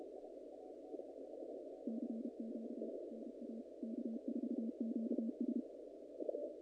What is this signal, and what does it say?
Morse code, heard as a keyed tone, about 30 wpm, 210 Hz. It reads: CQ NA K4QS
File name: contests / naqp-cw-k5tr-op-k5pi- -2100